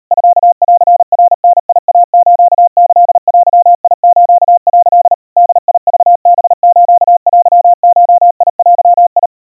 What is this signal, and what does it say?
Morse code (Japanese Wabun code), heard as a keyed tone, about 37 wpm, 700 Hz. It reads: ミンナダイスキデス」 ボクハステゴデ
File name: kawa / Morse sample / kamyu 140 cpm 700 Hz Crispy